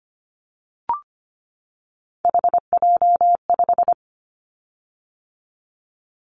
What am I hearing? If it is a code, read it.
Morse code, heard as a keyed tone, about 25 wpm, 700 Hz. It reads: HJ5
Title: Callsign Prefixes for Top 100 DXCC - 25wpm